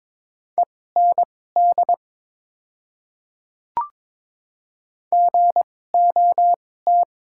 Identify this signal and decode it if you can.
Morse code, heard as a keyed tone, about 22 wpm, spaced as for 16 wpm, 690 Hz. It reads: END GOT
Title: 2x Character Spacing - Top 500 Words - 22wpm